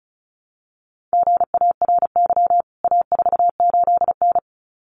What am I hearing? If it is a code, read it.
Morse code, heard as a keyed tone, about 34 wpm, 700 Hz. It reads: GARY A48N
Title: CWT Exchange 35wpm